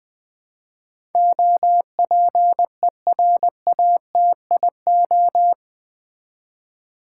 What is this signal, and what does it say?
Morse code, heard as a keyed tone, about 20 wpm, 700 Hz. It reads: OPERATIO
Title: Build A Word - 4 Letters At A Time - 11-Letter Words - 20wpm